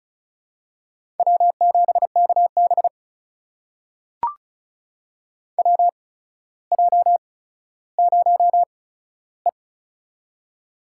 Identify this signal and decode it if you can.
Morse code, heard as a keyed tone, about 36 wpm, spaced as for 15 wpm, 700 Hz. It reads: W7KB WJ0E
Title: USA Call Signs - 4 Characters - 8x Character Spacing - 35wpm